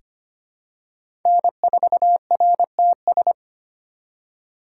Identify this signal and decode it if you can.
Morse code, heard as a keyed tone, about 25 wpm, 700 Hz. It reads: N4RTS
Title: USA Call Signs - 6x Character Spacing - 25wpm